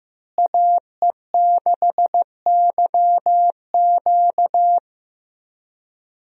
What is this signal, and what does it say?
Morse code, heard as a keyed tone, about 15 wpm, 700 Hz. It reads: AE6YQ
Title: USA Call Signs - 5 Characters - Repeated 3x - 15wpm